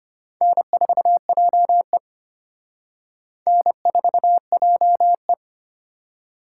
Morse code, heard as a keyed tone, about 30 wpm, 700 Hz. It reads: N4JE N4JE
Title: Speed Racing - USA 1x2 Call Signs 25wpm